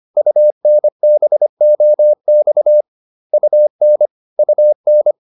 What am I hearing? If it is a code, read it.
Morse code, heard as a keyed tone, about 25 wpm, 600 Hz. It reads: UNBOX UN UN